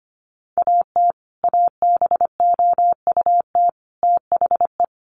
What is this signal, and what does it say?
Morse code, heard as a keyed tone, about 25 wpm, 700 Hz. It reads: AT ABOUT THE
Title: Sets of 5 Words Encore 25wpm